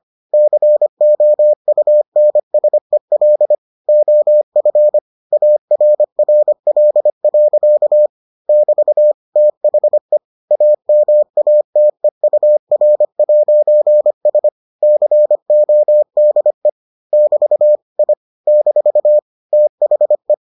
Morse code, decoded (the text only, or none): COUNSEL OF ARRL. <BT> THE AMATEUR'S CODE <BT> I - THE